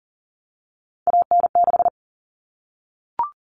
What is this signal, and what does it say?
Morse code, heard as a keyed tone, about 40 wpm, 700 Hz. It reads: AN6
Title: Common Cut Numbers - 3-Digit 40wpm